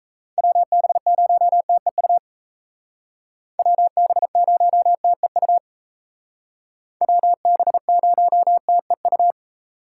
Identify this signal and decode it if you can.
Morse code, heard as a keyed tone, about 37 wpm, 700 Hz. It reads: WB0TEU WB0TEU WB0TEU